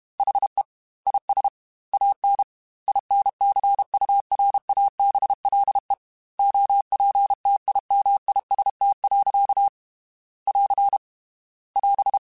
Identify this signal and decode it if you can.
Morse code, heard as a keyed tone, about 32 wpm, 800 Hz. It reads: HE IS AN INCURABLE OPTIMIST. <AR> <AS>
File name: Quotes / Quotes-mp3s / Optimist @30